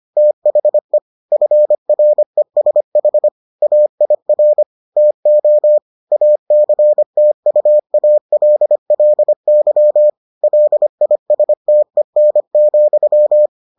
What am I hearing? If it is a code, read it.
Morse code, heard as a keyed tone, about 25 wpm, 600 Hz. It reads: THE FRESH AIR TO ACTUALLY LISTEN,